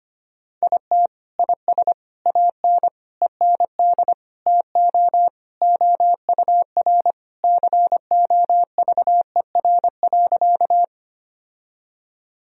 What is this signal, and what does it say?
Morse code, heard as a keyed tone, about 25 wpm, 700 Hz. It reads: IT IS AN END TO OUR COVER.
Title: Sentences from Top 1000 Words - Review Part 1 - 25wpm